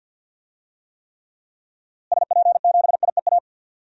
Morse code, heard as a keyed tone, about 52 wpm, 700 Hz. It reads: SP7IEA
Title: DXCC-100 Call Signs - 6 Characters - Repeated 3x - 50wpm